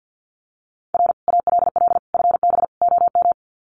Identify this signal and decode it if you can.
Morse code, heard as a keyed tone, about 50 wpm, 700 Hz. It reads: R ALL FB OM
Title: Common QSO Phrases 50wpm